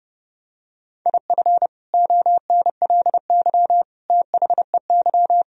Morse code, heard as a keyed tone, about 32 wpm, 700 Hz. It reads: IF ONLY THEY
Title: Sentences from Top 500 Words and Abbreviations 30wpm